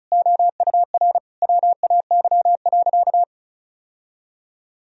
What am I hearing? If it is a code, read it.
Morse code, heard as a keyed tone, about 36 wpm, 700 Hz. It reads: OUR WAY.
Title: Sentences from Top 1000 Words - Review Part 1 - 35wpm